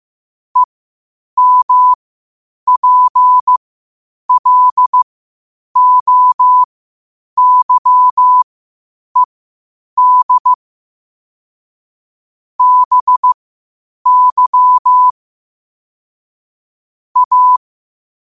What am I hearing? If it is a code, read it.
Morse code, heard as a keyed tone, about 15 wpm, spaced as for 8 wpm, 1000 Hz. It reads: EMPLOYED BY A
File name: qotdfiles / 8wpm